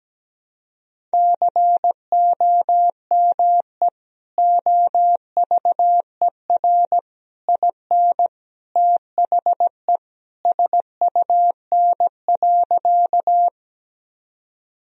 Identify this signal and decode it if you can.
Morse code, heard as a keyed tone, about 17 wpm, 700 Hz. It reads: COME OVER IN THE SUN.
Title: Sentences from Top 1000 Words - Review Part 1 - 17wpm